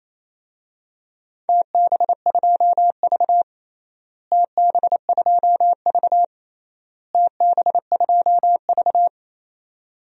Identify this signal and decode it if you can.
Morse code, heard as a keyed tone, about 28 wpm, 700 Hz. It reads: TB2V TB2V TB2V
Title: DXCC-100 Call Signs - 4 Characters - Repeated 3x - 28wpm